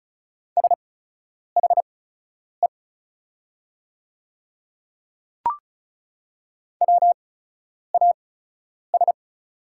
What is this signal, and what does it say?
Morse code, heard as a keyed tone, about 34 wpm, spaced as for 10 wpm, 690 Hz. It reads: SHE WAS
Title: 8x Character Spacing - Top 100 Words - 35wpm